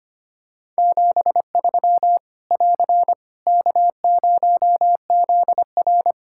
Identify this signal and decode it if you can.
Morse code, heard as a keyed tone, about 25 wpm, 700 Hz. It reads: 73 <AR> K0ZR
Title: Practice QSO - Exchange 4 - Ending - 25wpm